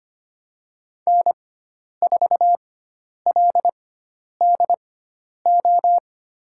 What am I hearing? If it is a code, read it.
Morse code, heard as a keyed tone, about 25 wpm, spaced as for 10 wpm, 700 Hz. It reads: N4LDO
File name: Lesson 16g - Letter c - Call Signs 25wpm